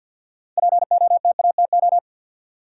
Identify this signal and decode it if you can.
Morse code, heard as a keyed tone, about 52 wpm, 700 Hz. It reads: POTATO
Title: Six Letter Words - 50wpm